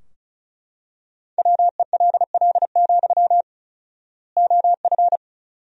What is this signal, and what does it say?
Morse code, heard as a keyed tone, about 36 wpm, 700 Hz. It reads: WELL, OF
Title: 4x Word-Spacing - Sentences from Top 500 Words - 35wpm